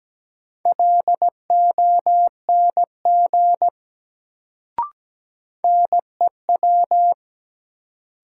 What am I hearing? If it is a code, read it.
Morse code, heard as a keyed tone, about 17 wpm, 700 Hz. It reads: LONG NEW
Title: Sets of 2 Words - New Word Drill - 17wpm